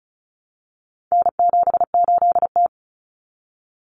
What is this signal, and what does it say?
Morse code, heard as a keyed tone, about 34 wpm, 700 Hz. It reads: N78T
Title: Common Cut Numbers - 4-Digit 35wpm